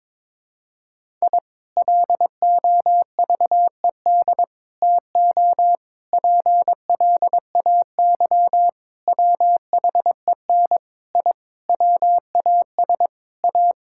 Morse code, heard as a keyed tone, about 22 wpm, 700 Hz. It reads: I LOVED TO PLAY WHEN I WAS A
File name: practice / 22wpm / Sentences with 4-Word Phrases - 22wpm